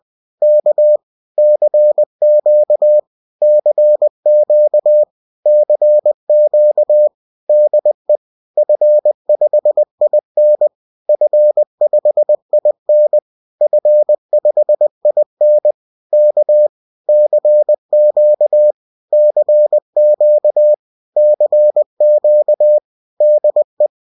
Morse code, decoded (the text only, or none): K CQ CQ CQ DE F5IN F5IN F5IN K CQ CQ CQ DE